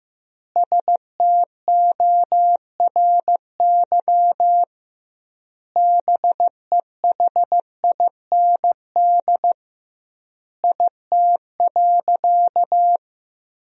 Morse code, heard as a keyed tone, about 15 wpm, 700 Hz. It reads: STORY BEHIND IT.
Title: 2x Word-Spacing - Sentences from Top 500 Words - 15wpm